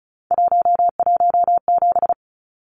Morse code, heard as a keyed tone, about 34 wpm, 700 Hz. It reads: 117
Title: Numbers - 3-Digit 35wpm